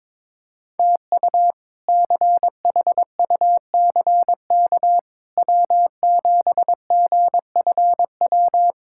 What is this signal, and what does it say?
Morse code, heard as a keyed tone, about 22 wpm, 700 Hz. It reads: TU CHUCK W7GFW
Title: K1USN SST QSO Practice - 22wpm